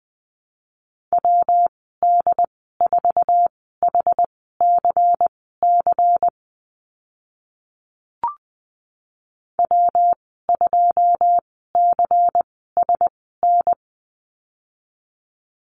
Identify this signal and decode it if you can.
Morse code, heard as a keyed tone, about 20 wpm, spaced as for 14 wpm, 700 Hz. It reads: WD4HCC W2CSN
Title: 161 Lesson 26g - Letter g - Call Signs 20wpm